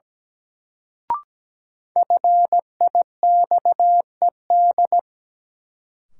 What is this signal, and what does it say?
Morse code, heard as a keyed tone, about 17 wpm, 700 Hz. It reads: FIXED
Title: Sentences from Top 1100 Words - New Word Drill Alt - 17wpm